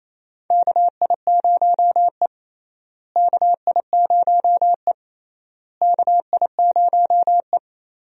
Morse code, heard as a keyed tone, about 28 wpm, 700 Hz. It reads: KI0E KI0E KI0E
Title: USA Call Signs - 4 Characters - Repeated 3x - 28wpm